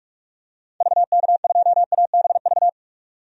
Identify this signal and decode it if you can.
Morse code, heard as a keyed tone, about 41 wpm, 700 Hz. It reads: VK2ABV
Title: DXCC-100 Call Signs - 6 Characters - 45wpm